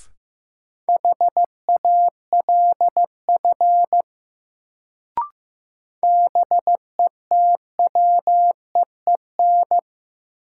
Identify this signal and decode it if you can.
Morse code, heard as a keyed tone, about 15 wpm, 700 Hz. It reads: HALF BETWEEN
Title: Sentences from Top 400 Words - New Word Drill - 15wpm